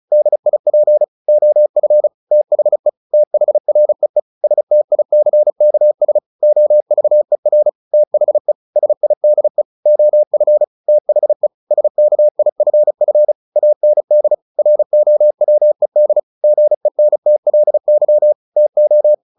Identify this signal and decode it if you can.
Morse code, heard as a keyed tone, about 34 wpm, 600 Hz. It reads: DIP OF THE THREE STICKS OVER THE SIDE OF THE SKIFF AND ROWED GENTLY TO
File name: the old man and the sea / the old man and the sea 1 35wpm